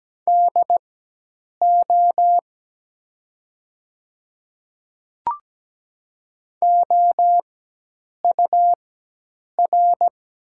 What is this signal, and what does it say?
Morse code, heard as a keyed tone, about 17 wpm, spaced as for 8 wpm, 700 Hz. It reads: DO OUR
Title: 4x Character Spacing - Top 300 Words - 17wpm